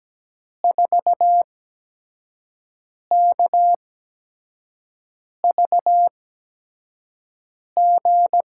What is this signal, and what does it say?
Morse code, heard as a keyed tone, about 17 wpm, spaced as for 5 wpm, 700 Hz. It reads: 4KVG
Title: USA Call Signs - 8x Character Spacing - 17wpm